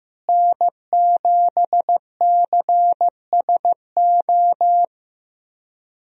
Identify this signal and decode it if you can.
Morse code, heard as a keyed tone, about 15 wpm, 700 Hz. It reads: N7CSO
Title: USA Call Signs - 5 Characters - Repeated 3x - 15wpm